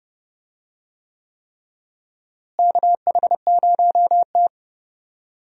Morse code, heard as a keyed tone, about 30 wpm, 700 Hz. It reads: KH0T
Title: USA Call Signs - 4 Characters - Repeated 3x - 30wpm